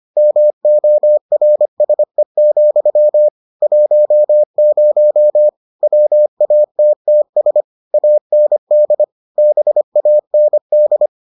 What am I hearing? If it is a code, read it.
Morse code, heard as a keyed tone, about 25 wpm, 600 Hz. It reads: MORSE, 10 WATTS AND BAND